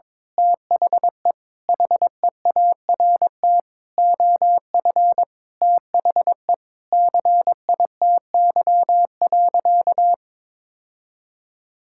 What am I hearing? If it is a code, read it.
Morse code, heard as a keyed tone, about 22 wpm, 700 Hz. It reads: THE HEART OF THE CITY.
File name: practice / 22wpm / Sentences with 4-Word Phrases - 22wpm